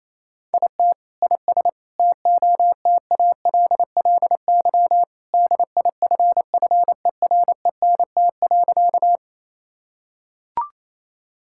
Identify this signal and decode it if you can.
Morse code, heard as a keyed tone, about 28 wpm, 700 Hz. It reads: IT IS TOTALLY DIFFERENT.
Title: Sentences from Top 1600 Words 28wpm